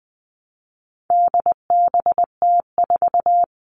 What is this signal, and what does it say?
Morse code, heard as a keyed tone, about 20 wpm, 700 Hz. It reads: DBT4
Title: All Cut Numbers - 4-Digit 20wpm